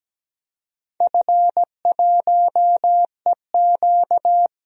Morse code, heard as a keyed tone, about 17 wpm, 700 Hz. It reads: F1EQ
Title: DXCC-100 Call Signs - 4 Characters - 17wpm